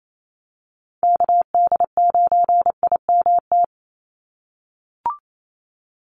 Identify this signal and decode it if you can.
Morse code, heard as a keyed tone, about 28 wpm, 700 Hz. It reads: KD9IMT
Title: USA Call Signs 28wpm